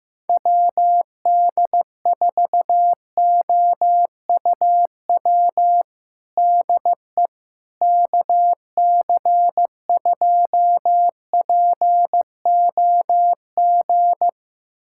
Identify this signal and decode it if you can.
Morse code, heard as a keyed tone, about 15 wpm, 700 Hz. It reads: WD4OUW DE KC2POG
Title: USA Callsign DE Callsign 15wpm